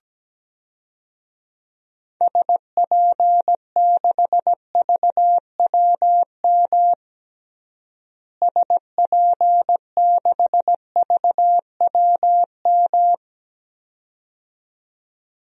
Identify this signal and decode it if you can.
Morse code, heard as a keyed tone, about 17 wpm, 700 Hz. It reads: SP6VWM SP6VWM
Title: DXCC-100 Call Signs - 6 Characters - Repeated 2x - 17wpm